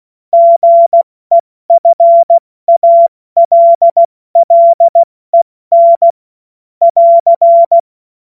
Morse code, heard as a keyed tone, about 16 wpm, 680 Hz. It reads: GEFALLEN <AR>
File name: Morsekurs / Witze / Witz 27. Juli 2021, 70 F80